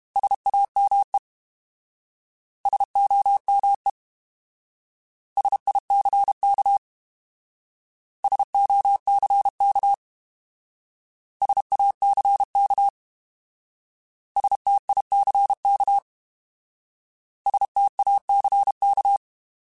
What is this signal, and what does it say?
Morse code, heard as a keyed tone, about 32 wpm, 800 Hz. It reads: SAME SOME SICK SOCK SACK STICK STACK
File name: Alphabetical / Alphabetical mp3s / Ss @30